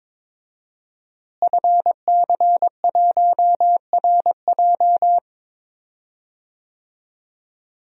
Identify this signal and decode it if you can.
Morse code, heard as a keyed tone, about 22 wpm, 700 Hz. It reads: FC1RJ
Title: DXCC-100 Call Signs - 5 Characters - Repeated 2x - 22wpm